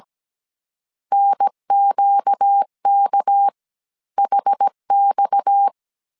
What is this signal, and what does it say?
Morse code, heard as a keyed tone, about 17 wpm, spaced as for 14 wpm, 800 Hz. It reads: NQKHX
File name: CW / DET / 1 6 1500C S A 16wpm 2635